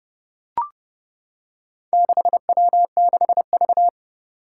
Morse code, heard as a keyed tone, about 28 wpm, 700 Hz. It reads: 6W6V
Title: Four Challenging Letter-Number Combinations 30wpm